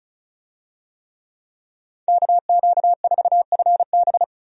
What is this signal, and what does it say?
Morse code, heard as a keyed tone, about 36 wpm, 700 Hz. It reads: KQ4FB
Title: USA Call Signs - 5 Characters - 35wpm